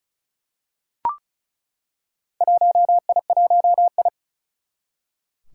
Morse code, heard as a keyed tone, about 36 wpm, 700 Hz. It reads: 1I1I
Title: Four Challenging Letter-Number Combinations 35wpm